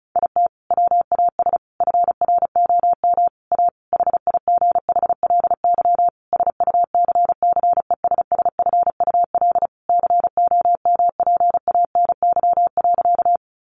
Morse code, heard as a keyed tone, about 34 wpm, 700 Hz. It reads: IT WAS FROM A HIGHLY SUCCESSFUL COMPANY.
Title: Sentences from Top 1100 Words 35wpm